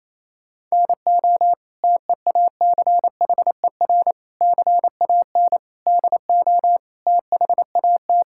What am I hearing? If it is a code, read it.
Morse code, heard as a keyed tone, about 28 wpm, 700 Hz. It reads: NO TEACHER CAN DO THAT
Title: Sentences from Top 1000 Words - Review Part 6 - 28wpm